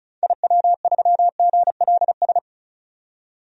Morse code, heard as a keyed tone, about 36 wpm, 700 Hz. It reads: IW3GLS
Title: DXCC-100 Call Signs - 6 Characters - 35wpm